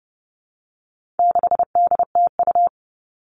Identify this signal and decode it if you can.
Morse code, heard as a keyed tone, about 30 wpm, 700 Hz. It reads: 6DTU
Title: All Cut Numbers - 4-Digit 30wpm